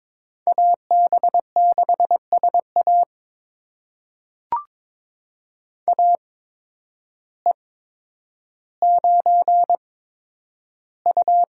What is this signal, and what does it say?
Morse code, heard as a keyed tone, about 22 wpm, spaced as for 10 wpm, 700 Hz. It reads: AB6SA AE9U
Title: USA Call Signs - 8x Character Spacing - 22wpm